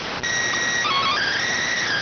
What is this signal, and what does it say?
SSTV (slow-scan television), PD 290.